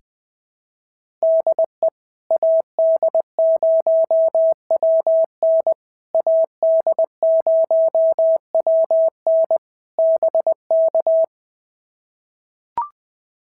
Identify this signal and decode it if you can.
Morse code, heard as a keyed tone, about 19 wpm, 650 Hz. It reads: DE AD0WN AD0WN BK
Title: POTA QSO Practice - 20wpm